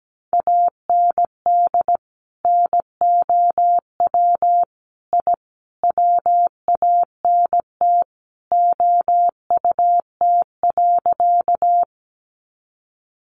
Morse code, heard as a keyed tone, about 17 wpm, 700 Hz. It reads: AND NOW I WANT OUT.
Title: A Sound Of Thunder 17wpm